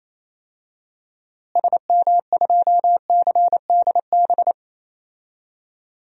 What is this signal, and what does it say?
Morse code, heard as a keyed tone, about 28 wpm, 700 Hz. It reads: SM2CDB
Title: DXCC-100 Call Signs - 6 Characters - Repeated 2x - 28wpm